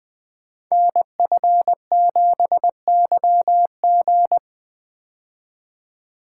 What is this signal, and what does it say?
Morse code, heard as a keyed tone, about 20 wpm, 700 Hz. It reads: NF7YG
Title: Code Groups - 5 Letter:Numbers - 20wpm